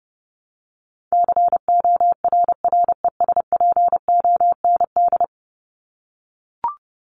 Morse code, heard as a keyed tone, about 30 wpm, 700 Hz. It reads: CORRESPOND
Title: ING Suffix - Top 100 Words with Base Words 30wpm